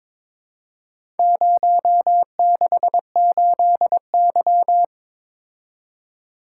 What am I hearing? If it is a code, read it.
Morse code, heard as a keyed tone, about 22 wpm, 700 Hz. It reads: 068Y
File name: practice / 22wpm / Four Letter-Number Combinations - 22wpm